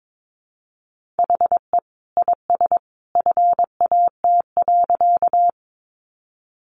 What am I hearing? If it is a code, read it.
Morse code, heard as a keyed tone, about 22 wpm, 700 Hz. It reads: HE IS FAT.
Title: Easy Phrases 22wpm